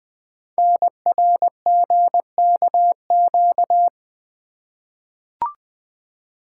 Morse code, heard as a keyed tone, about 20 wpm, 700 Hz. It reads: NRGKQ
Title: Code Groups - 5 Letters - 20wpm